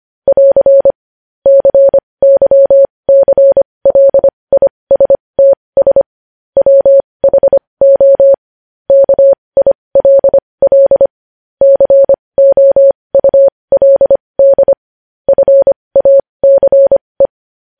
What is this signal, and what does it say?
Morse code, heard as a keyed tone, about 25 wpm, spaced as for 20 wpm, 550 Hz. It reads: <AR> CYCLISTS WHO KILL COULD FACE
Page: news headlines in morse code at 20 wpm